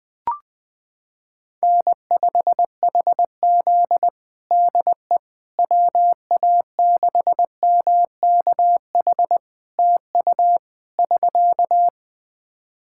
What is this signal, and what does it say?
Morse code, heard as a keyed tone, about 20 wpm, 700 Hz. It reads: N5HZ DE WA6MKH TU <SK>
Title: Practice QSO - Exchange 4 - Ending - 20wpm